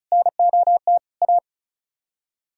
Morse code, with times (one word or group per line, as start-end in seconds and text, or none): 0.1-1.0 NOT
1.2-1.4 A